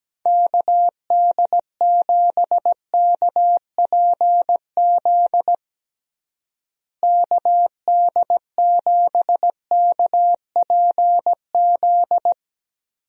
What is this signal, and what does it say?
Morse code, heard as a keyed tone, about 17 wpm, 700 Hz. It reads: KD7KPZ KD7KPZ
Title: USA Call Signs - 6 Characters - Repeated 2x - 17wpm